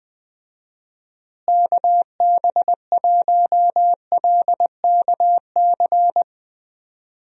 Morse code, heard as a keyed tone, about 20 wpm, 700 Hz. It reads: KB1LKC
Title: USA Call Signs - 6 Characters - Repeated 2x - 20wpm